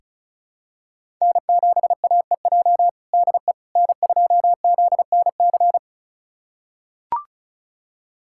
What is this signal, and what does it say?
Morse code, heard as a keyed tone, about 36 wpm, 700 Hz. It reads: N7AEJ DE N2ZNC
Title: USA Callsign DE Callsign 35wpm